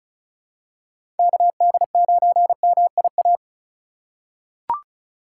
Morse code, heard as a keyed tone, about 34 wpm, 700 Hz. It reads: KD9MIA
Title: USA Call Signs - 2x Character Spacing - 35wpm